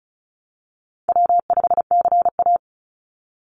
Morse code, heard as a keyed tone, about 34 wpm, 700 Hz. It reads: W5CA
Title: USA Call Signs 35wpm